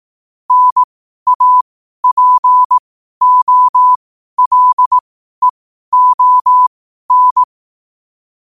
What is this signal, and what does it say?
Morse code, heard as a keyed tone, about 18 wpm, spaced as for 12 wpm, 1000 Hz. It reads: NAPOLEON
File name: qotdfiles / 12wpm